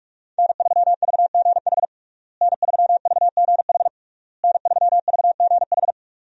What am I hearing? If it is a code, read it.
Morse code, heard as a keyed tone, about 46 wpm, 700 Hz. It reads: N3VGH N3VGH N3VGH
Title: USA Call Signs - 5 Characters - Repeated 3x - 45wpm